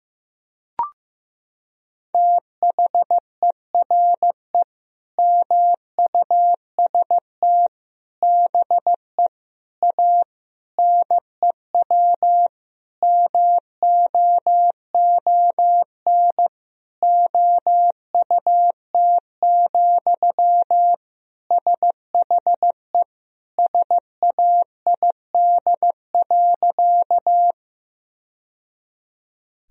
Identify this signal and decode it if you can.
Morse code, heard as a keyed tone, about 15 wpm, 700 Hz. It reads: THERE MUST BE A NEW MOON OUT, SHE SAID.
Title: Sentences from Top 1000 Words - Review Part 3 - 15wpm